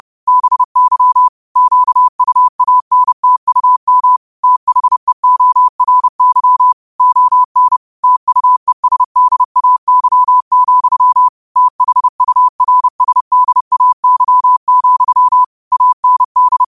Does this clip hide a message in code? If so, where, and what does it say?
Yes: Morse code from 0.3 to 16.6 s, heard as a keyed tone, about 30 wpm, 1000 Hz. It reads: BY QUANTUM THEORY ON TUESDAY, THURSDAY, AND